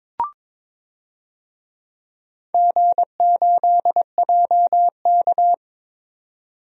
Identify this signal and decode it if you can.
Morse code, heard as a keyed tone, about 21 wpm, 700 Hz. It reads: G8JK
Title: DXCC-100 Call Signs - 4 Characters - 22wpm